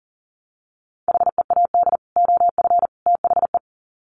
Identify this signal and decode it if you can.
Morse code, heard as a keyed tone, about 40 wpm, 700 Hz. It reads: HEAD OF THE
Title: Sets of 3 Words 40wpm